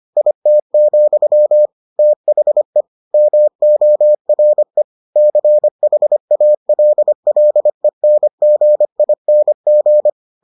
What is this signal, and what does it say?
Morse code, heard as a keyed tone, about 25 wpm, 600 Hz. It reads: IT, THE MORE CHALLENGING